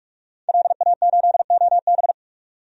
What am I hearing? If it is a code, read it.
Morse code, heard as a keyed tone, about 42 wpm, 700 Hz. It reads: LA8OB
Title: DXCC-100 Call Signs - 5 Characters - 45wpm